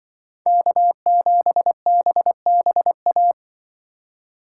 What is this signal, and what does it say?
Morse code, heard as a keyed tone, about 24 wpm, 700 Hz. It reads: K7BBA
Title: Speed Racing - USA 1x3 Call Signs 20wpm